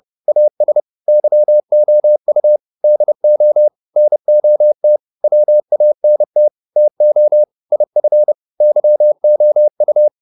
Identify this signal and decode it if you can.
Morse code, heard as a keyed tone, about 30 wpm, 600 Hz. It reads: AS YOU DO NOT WANT TO IF YOU